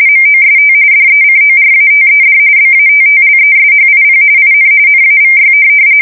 RTTY, about 45.45 baud. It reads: MOST DECORATED BATTLESHIP IN NAVY
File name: fsk170 / USS-New-Jersey